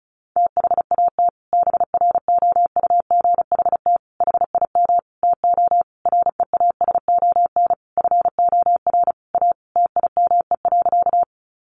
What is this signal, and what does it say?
Morse code, heard as a keyed tone, about 34 wpm, 700 Hz. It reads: THAT BROUGHT HIM TO REASON FOR A TIME.
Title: Sentences from Top 700 Words 35wpm